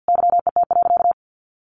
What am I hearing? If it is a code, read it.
Morse code, heard as a keyed tone, about 50 wpm, 700 Hz. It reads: YET.
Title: Sentences from Top 1300 Words 50wpm